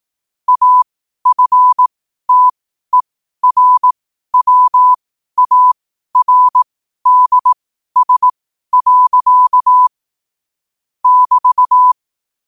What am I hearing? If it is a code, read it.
Morse code, heard as a keyed tone, about 18 wpm, spaced as for 12 wpm, 1000 Hz. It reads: AFTERWARDS. <BT>
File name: 12wpm